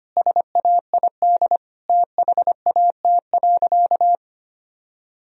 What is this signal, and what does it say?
Morse code, heard as a keyed tone, about 25 wpm, 700 Hz. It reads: SAID THAT.
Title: Sentences from Top 1000 Words - Review Part 1 - 25wpm